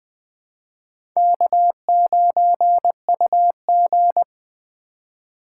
Morse code, heard as a keyed tone, about 20 wpm, 700 Hz. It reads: K9UG
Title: USA Call Signs - 4 Characters - 4x Character Spacing - 20wpm